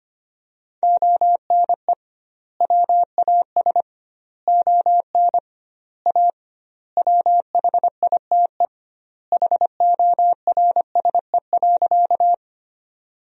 Morse code, heard as a keyed tone, about 25 wpm, 700 Hz. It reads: ONE WAS ON A WHITE HORSE.
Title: 2x Word-Spacing - Sentences from Top 300 Words - 25wpm